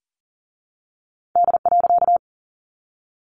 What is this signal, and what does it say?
Morse code, heard as a keyed tone, about 40 wpm, 700 Hz. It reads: D.
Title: A Sound Of Thunder 40wpm